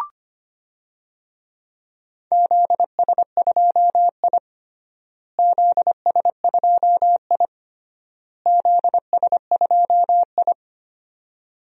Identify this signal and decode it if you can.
Morse code, heard as a keyed tone, about 25 wpm, 700 Hz. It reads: ZS2I ZS2I ZS2I
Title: DXCC-100 Call Signs - 4 Characters - Repeated 3x - 25wpm